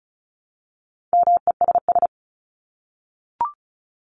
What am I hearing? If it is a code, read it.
Morse code, heard as a keyed tone, about 34 wpm, 700 Hz. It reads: MESS
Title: Top 1751-2000 Words 35wpm